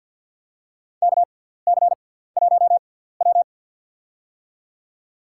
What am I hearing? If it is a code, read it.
Morse code, heard as a keyed tone, about 52 wpm, spaced as for 17 wpm, 700 Hz. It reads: KC1W
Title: USA Call Signs - 4 Characters - 6x Character Spacing - 50wpm